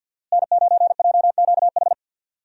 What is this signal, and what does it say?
Morse code, heard as a keyed tone, about 50 wpm, 700 Hz. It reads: N9JQH
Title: USA Call Signs - 5 Characters - Repeated 2x - 50wpm